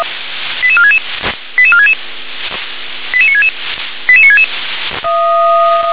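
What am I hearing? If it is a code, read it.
Five-tone selective call (ZVEI), heard as quick runs of five tones, tones of about 70 ms. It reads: 88466 88466 88966 88966
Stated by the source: ZVEI five-tone selective call 88465, received on an SDR.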